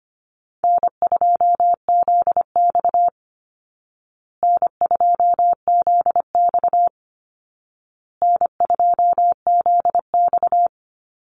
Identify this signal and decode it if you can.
Morse code, heard as a keyed tone, about 25 wpm, 700 Hz. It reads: N2ZX N2ZX N2ZX
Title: Call Signs Repeated 2x - 25wpm